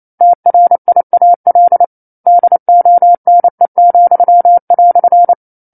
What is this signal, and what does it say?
Morse code, heard as a keyed tone, about 28 wpm, 700 Hz. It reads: TRIAL DONE,"
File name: 09-Chapter-09 28 WPM